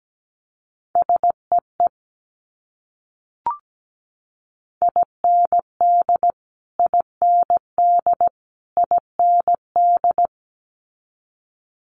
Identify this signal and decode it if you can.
Morse code, heard as a keyed tone, about 17 wpm, 700 Hz. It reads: SEE IND IND IND
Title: Multi-letter Phonemes Repeated 3x 17wpm